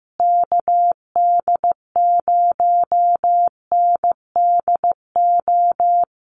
Morse code, heard as a keyed tone, about 15 wpm, 700 Hz. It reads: KD0NDO
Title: USA Call Signs Repeated 3x 15wpm